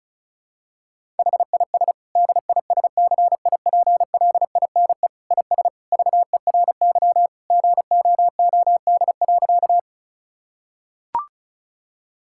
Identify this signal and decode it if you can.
Morse code, heard as a keyed tone, about 36 wpm, 700 Hz. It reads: HIS DISCIPLINE IS VERY GOOD.
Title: Sentences from Top 1000 Words - Review Part 6 - 35wpm